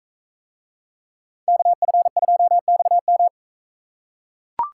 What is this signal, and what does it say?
Morse code, heard as a keyed tone, about 41 wpm, 700 Hz. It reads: KF2XM